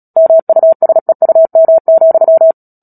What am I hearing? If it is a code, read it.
Morse code, heard as a keyed tone, about 37 wpm, 650 Hz. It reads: MUSEUM,
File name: Book Two / 10-Chapter 10 - Epilogue - 36 WPM